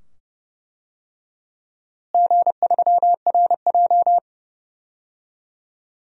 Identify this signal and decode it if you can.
Morse code, heard as a keyed tone, about 31 wpm, 700 Hz. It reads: G3RJ
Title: DXCC-100 Call Signs - 4 Characters - Repeated 2x - 30wpm